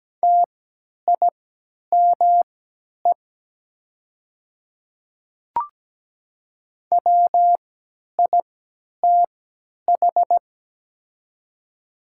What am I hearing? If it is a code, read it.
Morse code, heard as a keyed tone, about 17 wpm, spaced as for 10 wpm, 700 Hz. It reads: TIME WITH
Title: 3x Character Spacing - Top 100 Words - 17wpm